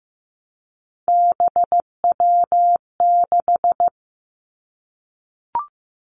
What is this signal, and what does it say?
Morse code, heard as a keyed tone, about 15 wpm, 700 Hz. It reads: BW6
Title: Three Challenging Letter-Number Combinations 15wpm